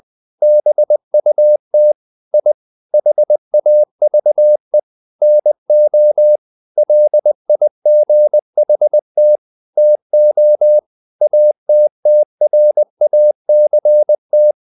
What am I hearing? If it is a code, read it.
Morse code, heard as a keyed tone, about 20 wpm, 600 Hz. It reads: BUT I HAVE NO LIGHT TO ATTRACT
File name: the old man and the sea / the old man and the sea part 1 20wpm